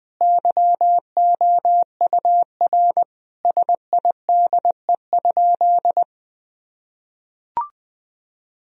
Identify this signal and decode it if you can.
Morse code, heard as a keyed tone, about 20 wpm, 700 Hz. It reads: YOUR SIDE?
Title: Sentences from Top 1000 Words - Review Part 1 - 20wpm